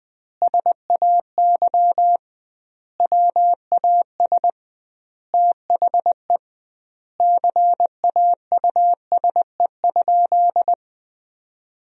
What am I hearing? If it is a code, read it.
Morse code, heard as a keyed tone, about 20 wpm, 700 Hz. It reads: SAY WAS THE CAUSE?
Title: 2x Word-Spacing - Sentences from Top 200 Words - 20wpm